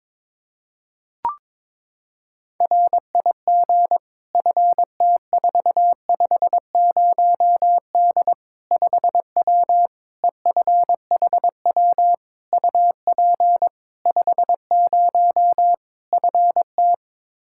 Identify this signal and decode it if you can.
Morse code, heard as a keyed tone, about 22 wpm, 700 Hz. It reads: RIG FT450D 5W EFHW UP 50 FT
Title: Practice QSO - Exchange 2 - Info - 22wpm